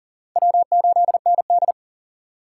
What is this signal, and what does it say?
Morse code, heard as a keyed tone, about 37 wpm, 700 Hz. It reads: W8ND
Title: USA Call Signs - 4 Characters - 2x Character Spacing - 40wpm